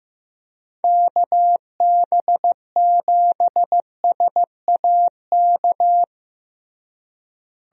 Morse code, heard as a keyed tone, about 15 wpm, 700 Hz. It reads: KB7SAK
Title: USA Call Signs - 6 Characters - Repeated 2x - 15wpm